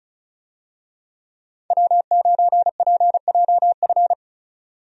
Morse code, heard as a keyed tone, about 34 wpm, 700 Hz. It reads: W9PJF
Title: USA Call Signs - 5 Characters - 35wpm